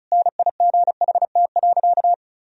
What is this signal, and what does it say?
Morse code, heard as a keyed tone, about 34 wpm, 700 Hz. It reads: NIGHT.
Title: Sentences from Top 1000 Words - Review Part 2 - 35wpm